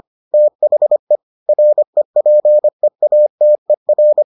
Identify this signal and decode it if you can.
Morse code, heard as a keyed tone, about 25 wpm, 600 Hz. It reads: THE REPEATER